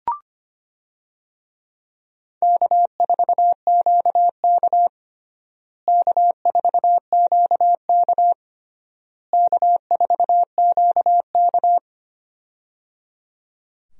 Morse code, heard as a keyed tone, about 25 wpm, 700 Hz. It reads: K4QK K4QK K4QK
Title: USA Call Signs - 4 Characters - Repeated 3x - 25wpm